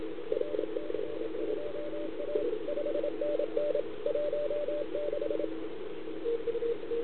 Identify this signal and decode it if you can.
Morse code, heard as a keyed tone, about 27 wpm, 580 Hz. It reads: RC2A 5NN 16
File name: CQWWCW2021 / Start2